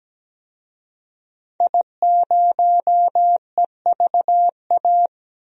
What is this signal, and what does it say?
Morse code, heard as a keyed tone, about 17 wpm, 700 Hz. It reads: I0EVA
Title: DXCC-100 Call Signs - 5 Characters - 17wpm